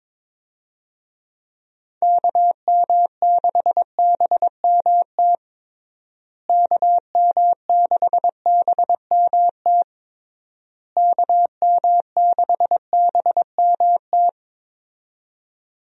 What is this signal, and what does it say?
Morse code, heard as a keyed tone, about 22 wpm, 700 Hz. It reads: KM6BMT KM6BMT KM6BMT
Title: USA Call Signs - 6 Characters - Repeated 3x - 22wpm